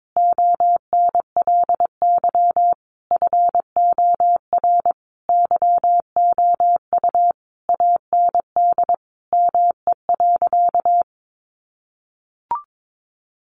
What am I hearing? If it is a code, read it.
Morse code, heard as a keyed tone, about 22 wpm, 700 Hz. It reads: ONLY FOR YOU AND ME.
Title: Sentences from Top 500 Words 22wpm